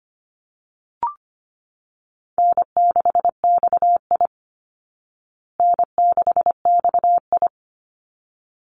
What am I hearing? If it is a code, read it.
Morse code, heard as a keyed tone, about 25 wpm, 700 Hz. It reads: N6XI N6XI
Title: Easy Call Signs Repeated 2x - 25wpm